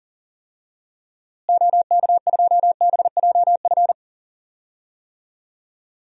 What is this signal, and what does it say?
Morse code, heard as a keyed tone, about 37 wpm, 700 Hz. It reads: OK2BJF
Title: DXCC-100 Call Signs - 6 Characters - Repeated 3x - 40wpm